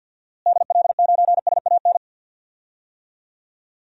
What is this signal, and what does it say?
Morse code, heard as a keyed tone, about 53 wpm, 700 Hz. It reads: DL9SAN
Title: DXCC-100 Call Signs - 6 Characters - Repeated 2x - 50wpm